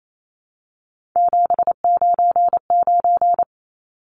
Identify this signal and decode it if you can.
Morse code, heard as a keyed tone, about 28 wpm, 700 Hz. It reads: 799
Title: Numbers - 3-Digit 28wpm